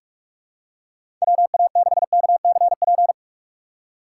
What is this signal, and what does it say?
Morse code, heard as a keyed tone, about 46 wpm, 700 Hz. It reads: WA6KCP
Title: USA Call Signs - 6 Characters - 6x Character Spacing - 45wpm